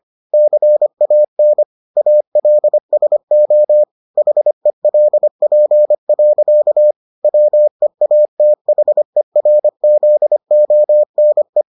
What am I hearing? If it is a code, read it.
Morse code, heard as a keyed tone, about 25 wpm, 600 Hz. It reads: CAN ALSO HELP. WEATHERZONE